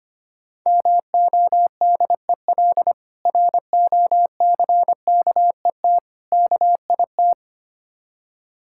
Morse code, heard as a keyed tone, about 25 wpm, 690 Hz. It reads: MODEL ROCKET KIT
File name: practice / 25wpm / Difficult Words with Hints - 25wpm